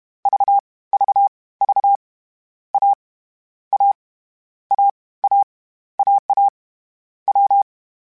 Morse code, heard as a keyed tone, about 31 wpm, 800 Hz. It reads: V V V A A A A AA W